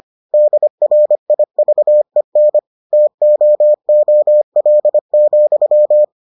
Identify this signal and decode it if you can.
Morse code, heard as a keyed tone, about 25 wpm, 600 Hz. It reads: DRIVEN TOOL,